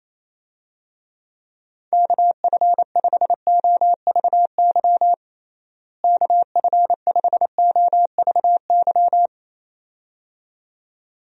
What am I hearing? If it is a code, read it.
Morse code, heard as a keyed tone, about 28 wpm, 700 Hz. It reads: KF5OVY KF5OVY
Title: USA Call Signs - 6 Characters - Repeated 2x - 28wpm